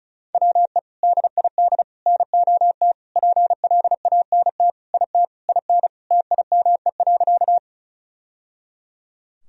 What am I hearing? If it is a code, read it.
Morse code, heard as a keyed tone, about 36 wpm, 700 Hz. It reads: WE DID NOT PLANT IT IN TIME.
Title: Sentences from Top 1000 Words - Review Part 1 - 35wpm